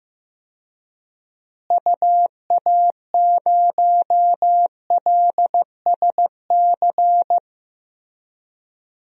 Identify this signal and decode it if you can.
Morse code, heard as a keyed tone, about 15 wpm, 700 Hz. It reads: UA0LSC
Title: DXCC-100 Call Signs - 6 Characters - Repeated 3x - 15wpm